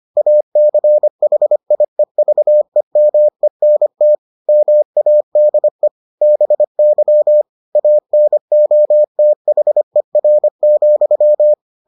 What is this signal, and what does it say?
Morse code, heard as a keyed tone, about 25 wpm, 600 Hz. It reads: ACHIEVEMENT MADE BY ANOTHER,